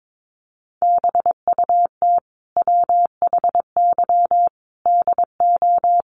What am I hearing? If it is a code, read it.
Morse code, heard as a keyed tone, about 22 wpm, 700 Hz. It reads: BUT WHY DO
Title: Sentences from Top 200 Words 22wpm